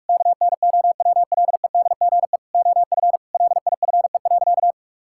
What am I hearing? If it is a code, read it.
Morse code, heard as a keyed tone, about 46 wpm, 700 Hz. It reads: KNOWLEDGE OF LIFE.
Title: Sentences from Top 1000 Words - Review Part 6 - 45wpm